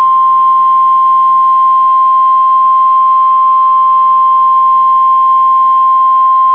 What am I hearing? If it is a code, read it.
Emergency Alert System (SAME) tones (the 1050 Hz attention tone). It is heard from 1.2 to 5.3 s.